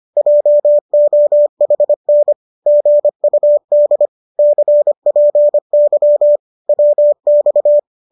Morse code, heard as a keyed tone, about 25 wpm, 590 Hz. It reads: JOHN GUD CPY WX